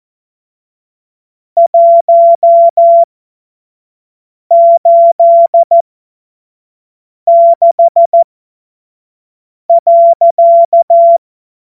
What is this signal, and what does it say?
Morse code, heard as a keyed tone, about 14 wpm, spaced as for 5 wpm, 680 Hz. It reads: 186.